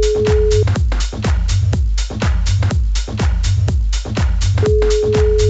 Ringback tone (Europe, 425 Hz).